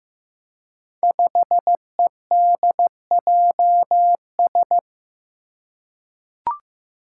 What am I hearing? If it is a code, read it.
Morse code, heard as a keyed tone, about 15 wpm, 700 Hz. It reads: 5EDJS
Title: Five Challenging Letter-Number Combinations 15wpm